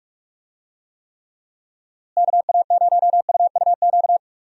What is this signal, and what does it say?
Morse code, heard as a keyed tone, about 41 wpm, 700 Hz. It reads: KA0UUQ
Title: USA Call Signs - 6 Characters - Repeated 3x - 45wpm